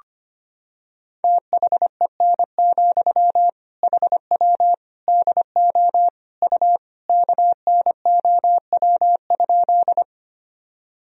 Morse code, heard as a keyed tone, about 25 wpm, 700 Hz. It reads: THEN, HW DO U KNOW?
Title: Sentences from Top 100 Words and Abbreviations 25wpm